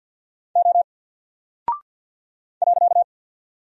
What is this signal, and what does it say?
Morse code, heard as a keyed tone, about 48 wpm, 700 Hz. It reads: O .
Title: Single Character (Letter:Number:Punctuation:Prosign) 50wpm